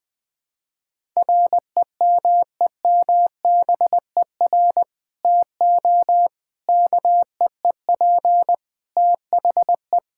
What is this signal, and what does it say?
Morse code, heard as a keyed tone, about 20 wpm, 700 Hz. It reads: REMEMBER TO KEEP THE
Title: Sentences with 3-Word Phrases - Spoken Beginning - 20wpm